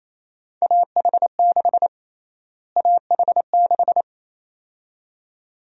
Morse code, heard as a keyed tone, about 28 wpm, 700 Hz. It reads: AH6 AH6
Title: USA Call Sign Prefixes - Repeated 2x - 28wpm